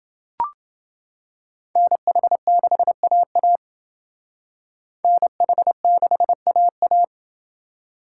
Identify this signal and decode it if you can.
Morse code, heard as a keyed tone, about 28 wpm, 700 Hz. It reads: NH6AA NH6AA